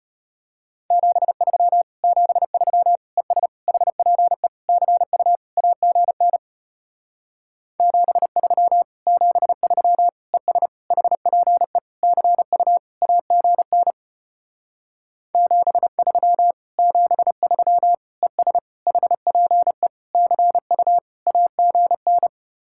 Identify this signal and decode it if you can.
Morse code, heard as a keyed tone, about 33 wpm, 700 Hz. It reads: 73 73 ES HPE CU AGN 73 73 ES HPE CU AGN 73 73 ES HPE CU AGN